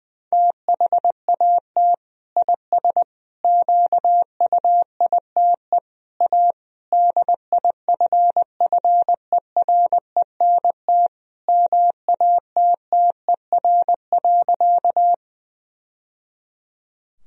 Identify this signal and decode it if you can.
Morse code, heard as a keyed tone, about 20 wpm, 700 Hz. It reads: THAT IS QUITE A DIFFERENT MATTER.
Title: Sentences from Top 1000 Words - Review Part 3 - 20wpm